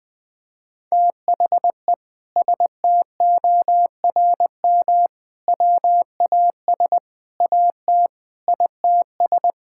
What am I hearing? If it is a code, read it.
Morse code, heard as a keyed tone, about 21 wpm, 700 Hz. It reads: THE STORM WAS AT ITS
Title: Sentences from Top 1600 Words 20wpm